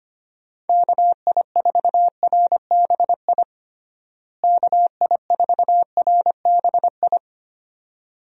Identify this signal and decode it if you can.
Morse code, heard as a keyed tone, about 25 wpm, 700 Hz. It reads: KI4RBI KI4RBI
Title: USA Call Signs - 6 Characters - Repeated 2x - 25wpm